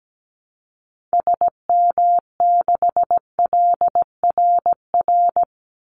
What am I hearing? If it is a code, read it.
Morse code, heard as a keyed tone, about 17 wpm, 700 Hz. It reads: SM6LRR
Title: Call Signs Repeated 2x - 17wpm